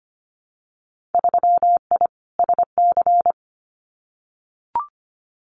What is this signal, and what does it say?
Morse code, heard as a keyed tone, about 25 wpm, 700 Hz. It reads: 3I SC
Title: Winter Field Day Exchanges 25wpm